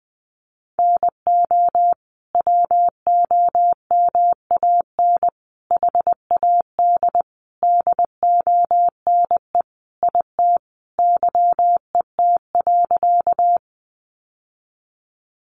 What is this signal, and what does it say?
Morse code, heard as a keyed tone, about 20 wpm, 700 Hz. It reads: NO WOMAN HAD DONE IT YET.
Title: Sentences from Top 1100 Words 20wpm